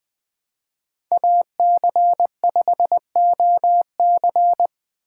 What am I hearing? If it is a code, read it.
Morse code, heard as a keyed tone, about 20 wpm, 700 Hz. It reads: AC5OC
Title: 2023 POTA Hunter Call Signs - 20wpm